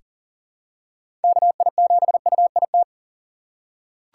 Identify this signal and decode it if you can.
Morse code, heard as a keyed tone, about 40 wpm, 700 Hz. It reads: KI7UIT
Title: USA Call Signs - 6 Characters - 2x Character Spacing - 40wpm